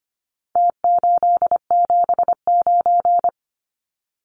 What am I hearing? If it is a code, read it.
Morse code, heard as a keyed tone, about 25 wpm, 700 Hz. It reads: T879
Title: Numbers - Contest Style - 4-Digit 25wpm